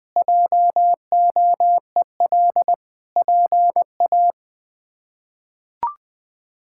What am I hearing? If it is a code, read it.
Morse code, heard as a keyed tone, about 21 wpm, 700 Hz. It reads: JOEL PA
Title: K1USN SST QSO Practice - 20wpm